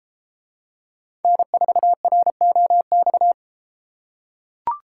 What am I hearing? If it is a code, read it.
Morse code, heard as a keyed tone, about 33 wpm, 700 Hz. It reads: N4ROX